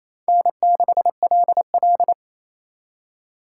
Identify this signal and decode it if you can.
Morse code, heard as a keyed tone, about 35 wpm, 700 Hz. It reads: N6LL
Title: Speed Racing - USA 1x2 Call Signs 28wpm